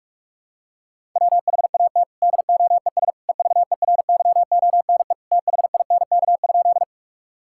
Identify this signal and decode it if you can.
Morse code, heard as a keyed tone, about 41 wpm, 700 Hz. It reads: WHAT DOES EVERYONE THINK?
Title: Sentences from Top 1000 Words - Review Part 5 - 45wpm